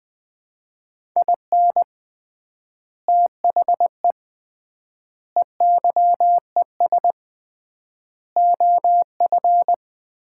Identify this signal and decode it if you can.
Morse code, heard as a keyed tone, about 20 wpm, 700 Hz. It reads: IN THE EYES OF
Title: 3x Word-Spacing - Sets of 5 Words - 20wpm